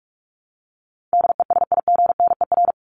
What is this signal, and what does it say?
Morse code, heard as a keyed tone, about 44 wpm, 700 Hz. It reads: DESIGNER
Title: Top 1501-1750 Words 45wpm